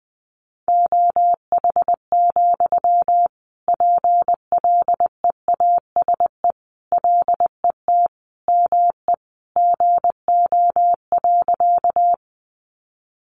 Morse code, heard as a keyed tone, about 20 wpm, 700 Hz. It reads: OH, PLEASE LET ME GO.
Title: Sentences from Top 1500 Words 20wpm